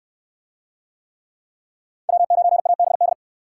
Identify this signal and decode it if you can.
Morse code, heard as a keyed tone, about 60 wpm, 700 Hz. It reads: W9ABD